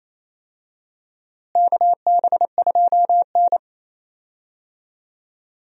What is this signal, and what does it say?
Morse code, heard as a keyed tone, about 27 wpm, 700 Hz. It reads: KB2N
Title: USA Call Signs - 4 Characters - Repeated 3x - 28wpm